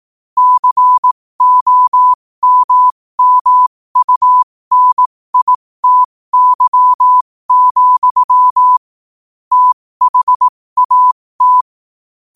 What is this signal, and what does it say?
Morse code, heard as a keyed tone, about 18 wpm, spaced as for 15 wpm, 1000 Hz. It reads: COMMUNITY, THAT
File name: qotdfiles / 15wpm